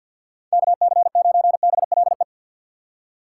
Morse code, heard as a keyed tone, about 53 wpm, 700 Hz. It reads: KC9BLE
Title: USA Call Signs - 6 Characters - 8x Character Spacing - 50wpm